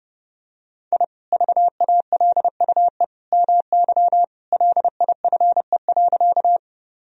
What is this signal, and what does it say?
Morse code, heard as a keyed tone, about 30 wpm, 700 Hz. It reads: I VALUE MY LIFE.
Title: Sentences from Top 1000 Words - Review Part 3 - 30wpm